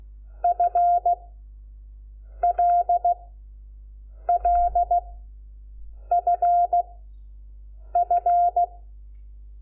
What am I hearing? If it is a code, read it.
Morse code, heard as a keyed tone, about 16 wpm, 700 Hz. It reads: F L L F F
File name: Lecciones / CLASE Nº5